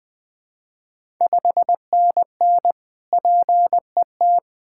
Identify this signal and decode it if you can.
Morse code, heard as a keyed tone, about 21 wpm, 700 Hz. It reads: 5NN PET
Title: Ontario QSO Party Exchange 20wpm